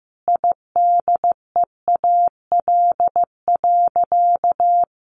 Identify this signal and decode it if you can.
Morse code, heard as a keyed tone, about 15 wpm, 700 Hz. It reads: IDEAL.
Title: Sentences from Top 1400 Words 15wpm